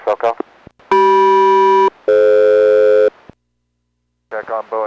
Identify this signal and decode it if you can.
Aircraft SELCAL, heard as two pairs of tones, with speech. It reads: CM-DF